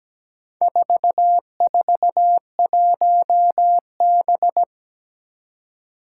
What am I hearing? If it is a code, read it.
Morse code, heard as a keyed tone, about 17 wpm, 700 Hz. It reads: 441B
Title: Four Challenging Letter-Number Combinations 17wpm